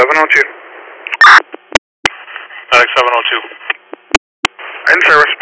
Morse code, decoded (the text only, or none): none